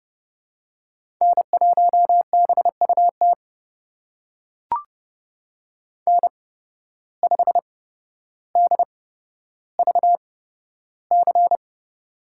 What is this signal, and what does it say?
Morse code, heard as a keyed tone, about 30 wpm, spaced as for 13 wpm, 700 Hz. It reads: N1BUT N5DVC
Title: USA Call Signs - 5 Characters - 8x Character Spacing - 30wpm